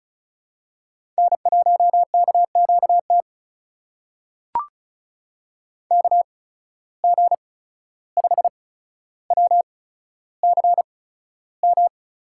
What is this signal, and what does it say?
Morse code, heard as a keyed tone, about 34 wpm, spaced as for 14 wpm, 700 Hz. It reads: N1KQT KG5WCM
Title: USA Call Signs - 8x Character Spacing - 35wpm